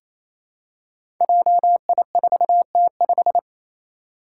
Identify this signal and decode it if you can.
Morse code, heard as a keyed tone, about 28 wpm, 700 Hz. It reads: JI4T5
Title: Five Challenging Letter-Number Combinations 28wpm